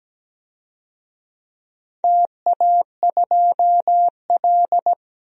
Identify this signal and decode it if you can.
Morse code, heard as a keyed tone, about 17 wpm, 700 Hz. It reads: TA2L
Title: DXCC-100 Call Signs - 4 Characters - 17wpm